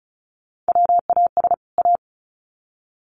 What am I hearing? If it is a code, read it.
Morse code, heard as a keyed tone, about 34 wpm, 700 Hz. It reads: WAS A
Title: Sets of 2 Words 35wpm